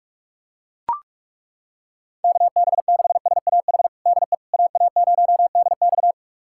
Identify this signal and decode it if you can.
Morse code, heard as a keyed tone, about 42 wpm, 700 Hz. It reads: KB6SAH DE AA0DX
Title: USA Callsign DE Callsign 45wpm